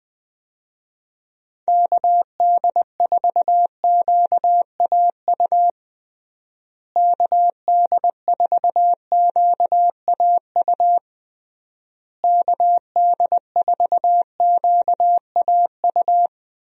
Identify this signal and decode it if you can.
Morse code, heard as a keyed tone, about 20 wpm, 700 Hz. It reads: KD4QAU KD4QAU KD4QAU
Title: USA Call Signs - 6 Characters - Repeated 3x - 20wpm